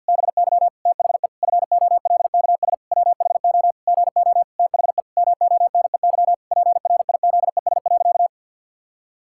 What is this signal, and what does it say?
Morse code, heard as a keyed tone, about 52 wpm, 700 Hz. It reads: BY THE FOLKS WHO GO THE MONEY PRIZES.
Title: Pangram Challenge v2 - 50wpm